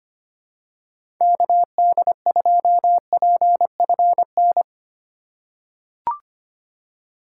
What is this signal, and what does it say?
Morse code, heard as a keyed tone, about 25 wpm, 700 Hz. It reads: KD2PFN
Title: USA Call Signs - 6 Characters - 3x Character Spacing - 25wpm